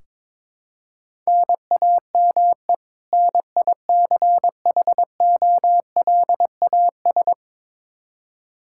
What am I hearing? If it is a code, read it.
Morse code, heard as a keyed tone, about 22 wpm, 700 Hz. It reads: NAME NICHOLAS
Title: K1USN SST QSO Practice 22wpm